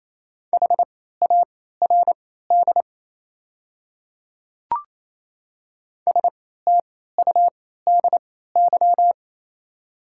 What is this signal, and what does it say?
Morse code, heard as a keyed tone, about 28 wpm, spaced as for 16 wpm, 700 Hz. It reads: HARD STUDY
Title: 3x Character Spacing - Top 300 Words - 28wpm